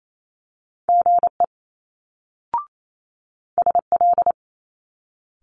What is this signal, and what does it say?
Morse code, heard as a keyed tone, about 28 wpm, 700 Hz. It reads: GE SL
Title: Multi-letter Phonemes Quick Fire 28wpm